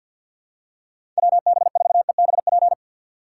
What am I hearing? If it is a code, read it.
Morse code, heard as a keyed tone, about 47 wpm, 700 Hz. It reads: WB4EBP
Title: USA Call Signs - 6 Characters - 4x Character Spacing - 50wpm